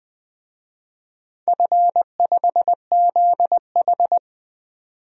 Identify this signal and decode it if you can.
Morse code, heard as a keyed tone, about 20 wpm, 700 Hz. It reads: F5ZH
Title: DXCC-100 Call Signs - 4 Characters - 20wpm